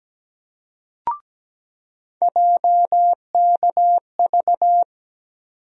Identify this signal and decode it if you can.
Morse code, heard as a keyed tone, about 17 wpm, 700 Hz. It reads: JKV
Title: USA Call Sign Postfixes - 3 Letters 17wpm